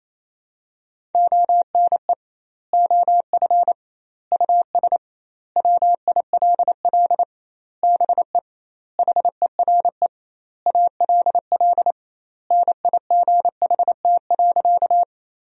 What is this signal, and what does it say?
Morse code, heard as a keyed tone, about 27 wpm, 700 Hz. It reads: ONE OF US WILL BE HERE ALL NIGHT.
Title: 2x Word-Spacing - Sentences from Top 300 Words - 28wpm